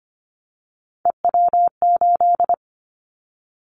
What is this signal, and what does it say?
Morse code, heard as a keyed tone, about 25 wpm, 700 Hz. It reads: EW8
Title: Callsign Prefixes for Top 50 DXCC - 25wpm